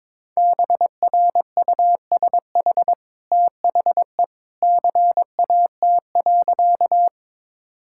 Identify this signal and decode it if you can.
Morse code, heard as a keyed tone, about 21 wpm, 700 Hz. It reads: BRUSH THE CAT.
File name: practice / 22wpm / Easy Sentences with Hints - 22wpm